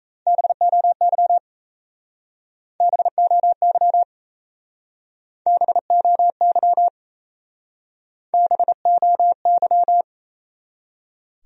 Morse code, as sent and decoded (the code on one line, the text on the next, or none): -... --- -.-- / -... --- -.-- / -... --- -.-- / -... --- -.--
BOY BOY BOY BOY